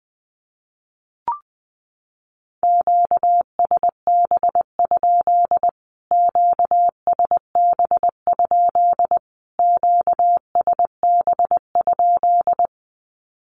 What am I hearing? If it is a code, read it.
Morse code, heard as a keyed tone, about 20 wpm, 700 Hz. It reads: QSB? QSB? QSB?
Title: Q-Codes Repeated 3x 20wpm